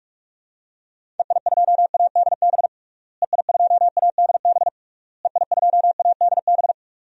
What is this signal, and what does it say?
Morse code, heard as a keyed tone, about 41 wpm, 700 Hz. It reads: EI2ADB EI2ADB EI2ADB
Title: DXCC-100 Call Signs - 6 Characters - Repeated 3x - 45wpm